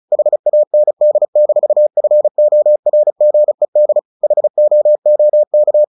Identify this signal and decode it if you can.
Morse code, heard as a keyed tone, about 34 wpm, 600 Hz. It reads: HAND-FORGED HOOK
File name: the old man and the sea / the old man and the sea 1 35wpm